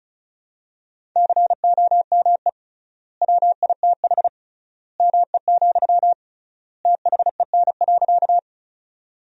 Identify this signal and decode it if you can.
Morse code, heard as a keyed tone, about 36 wpm, 700 Hz. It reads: COME WITH ME, THEN.
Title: 3x Word-Spacing - Sentences from Top 200 Words - 35wpm